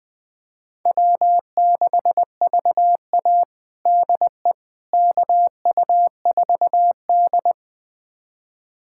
Morse code, heard as a keyed tone, about 21 wpm, 700 Hz. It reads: W6VA DE KU4D
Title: USA Callsign DE Callsign - 4 Characters - 20wpm